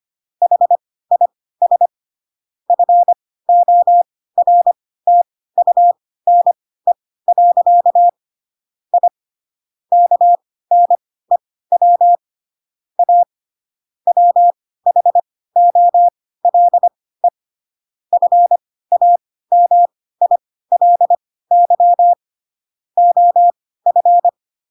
Morse code, heard as a keyed tone, about 25 wpm, spaced as for 16 wpm, 700 Hz. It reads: HIS FORTUNE. I KNEW A WHOLE FAMILY OF
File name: Pinocchio / Pinocchio ch3 10 25 ik1hge